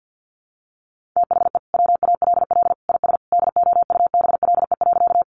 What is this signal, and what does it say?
Morse code, heard as a keyed tone, about 50 wpm, 700 Hz. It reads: THE WALL IS DOUBLE.
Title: Sentences from Top 1400 Words 50wpm